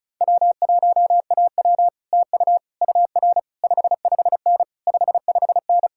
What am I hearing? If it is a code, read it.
Morse code, heard as a keyed tone, about 37 wpm, 700 Hz. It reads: W1AW TU UR 55N 55N
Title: Speed-Racing Common QSO Phrases Encore - 30wpm